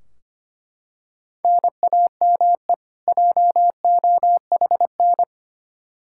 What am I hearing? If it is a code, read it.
Morse code, heard as a keyed tone, about 25 wpm, 700 Hz. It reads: NAME JOHN
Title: Practice QSO - Exchange 1 - Beginning - 25wpm